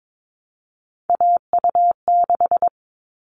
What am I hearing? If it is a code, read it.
Morse code, heard as a keyed tone, about 22 wpm, 700 Hz. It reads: AU6
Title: All Cut Numbers - 3-Digit 22wpm